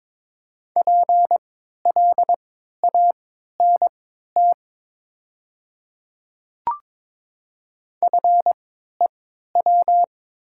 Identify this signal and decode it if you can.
Morse code, heard as a keyed tone, about 22 wpm, spaced as for 13 wpm, 700 Hz. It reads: PLANT FEW
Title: 3x Character Spacing - Top 300 Words - 22wpm